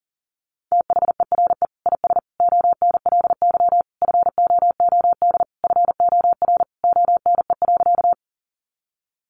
Morse code, heard as a keyed tone, about 40 wpm, 700 Hz. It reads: THERE IS ONLY FOOD FOR ONE.
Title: Sentences from Top 300 Words 40wpm